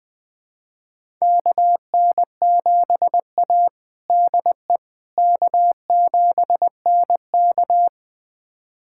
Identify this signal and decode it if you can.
Morse code, heard as a keyed tone, about 21 wpm, 700 Hz. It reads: KN7A DE K7NK
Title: USA Callsign DE Callsign - 4 Characters - 20wpm